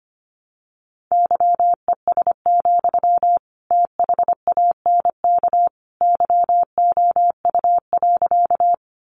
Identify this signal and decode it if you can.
Morse code, heard as a keyed tone, about 25 wpm, 700 Hz. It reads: YES, THANK YOU.
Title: Sentences from Top 600 Words 25wpm